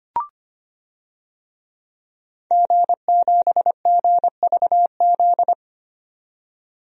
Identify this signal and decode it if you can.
Morse code, heard as a keyed tone, about 25 wpm, 700 Hz. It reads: G7GVZ
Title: DXCC-100 Call Signs - 5 Characters - 25wpm